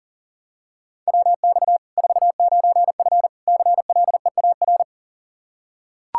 Morse code, heard as a keyed tone, about 40 wpm, 700 Hz. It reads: WX 49F CLEAR
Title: Practice QSO - Exchange 2 - Info - 40wpm